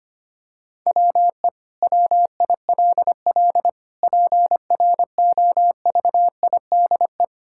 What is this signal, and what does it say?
Morse code, heard as a keyed tone, about 25 wpm, 700 Hz. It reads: WE WILL PROVIDE
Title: Sentences from Top 1000 Words - Review Part 5 - 25wpm